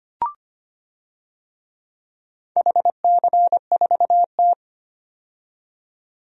Morse code, heard as a keyed tone, about 25 wpm, 700 Hz. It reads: HC4T
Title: DXCC-100 Call Signs - 4 Characters - 25wpm